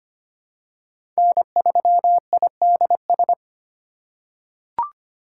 Morse code, heard as a keyed tone, about 25 wpm, 700 Hz. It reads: N3IDS
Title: USA Call Signs - 3x Character Spacing - 25wpm